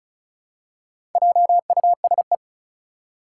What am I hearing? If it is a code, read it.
Morse code, heard as a keyed tone, about 36 wpm, 700 Hz. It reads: JUSE
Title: Four Challenging Letter-Number Combinations 35wpm